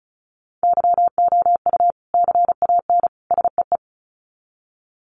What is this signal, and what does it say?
Morse code, heard as a keyed tone, about 34 wpm, 700 Hz. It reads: YOU CAN SEE
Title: Sets of 3 Words 35wpm